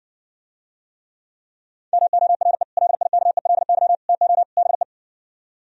Morse code, heard as a keyed tone, about 60 wpm, 700 Hz. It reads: MORE LIKELY TO BE